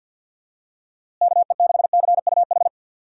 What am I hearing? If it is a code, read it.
Morse code, heard as a keyed tone, about 51 wpm, 700 Hz. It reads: KE6CUH
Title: USA Call Signs - 6 Characters - Repeated 3x - 50wpm